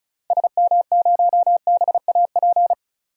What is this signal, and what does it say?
Morse code, heard as a keyed tone, about 34 wpm, 700 Hz. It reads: SM0BAP
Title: DXCC-100 Call Signs - 6 Characters - 35wpm